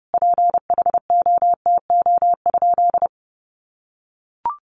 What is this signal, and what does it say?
Morse code, heard as a keyed tone, about 28 wpm, 700 Hz. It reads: PHOTO?
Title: Sentences from Top 1200 Words 30wpm